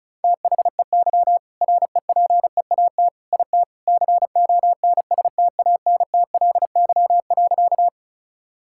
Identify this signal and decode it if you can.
Morse code, heard as a keyed tone, about 36 wpm, 700 Hz. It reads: THEY REPEAT IT CONSTANTLY.
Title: Sentences from Top 1600 Words 35wpm